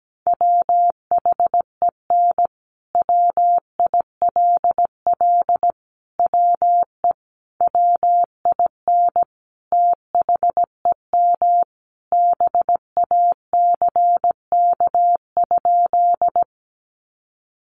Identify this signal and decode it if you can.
Morse code, heard as a keyed tone, about 17 wpm, 700 Hz. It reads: WHEN WILL WE WIN THEM BACK?
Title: Sentences from Top 1500 Words 17wpm